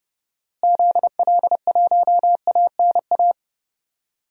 Morse code, heard as a keyed tone, about 30 wpm, 700 Hz. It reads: ZL1ANA
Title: DXCC-100 Call Signs - 6 Characters - Repeated 3x - 30wpm